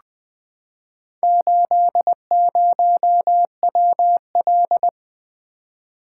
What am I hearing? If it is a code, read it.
Morse code, heard as a keyed tone, about 20 wpm, 700 Hz. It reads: 80WL
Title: Four Letter-Number Combinations - 20wpm